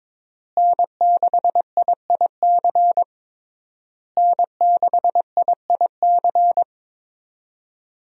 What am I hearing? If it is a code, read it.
Morse code, heard as a keyed tone, about 22 wpm, 700 Hz. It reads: N6IIC N6IIC
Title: USA Call Signs - 5 Characters - Repeated 3x - 22wpm